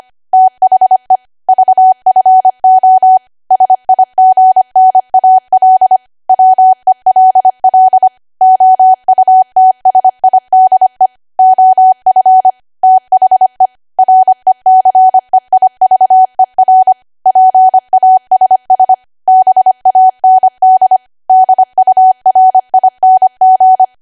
Morse code, heard as a keyed tone, about 25 wpm, 750 Hz. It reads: THE VFO SIGNAL WELL OUTSIDE OF THE RECEIVER PASS BAND DURING
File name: Archive / 25 WPM / 25WPM